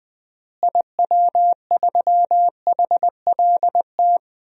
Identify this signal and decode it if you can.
Morse code, heard as a keyed tone, about 21 wpm, 700 Hz. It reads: IW3HLT
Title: DXCC-100 Call Signs - 6 Characters - Repeated 3x - 20wpm